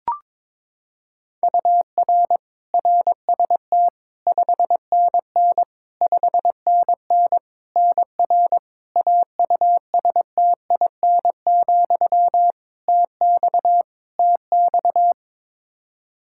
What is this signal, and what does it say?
Morse code, heard as a keyed tone, about 22 wpm, 700 Hz. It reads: UR RST 5NN 5NN NR AUSTIN, TX TX
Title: Practice QSO - Exchange 1 - Beginning - 22wpm